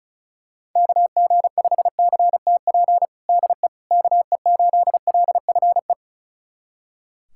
Morse code, heard as a keyed tone, about 36 wpm, 700 Hz. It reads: KG5CTP DE KE8LFE
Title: USA Callsign DE Callsign - 6 Characters - 35wpm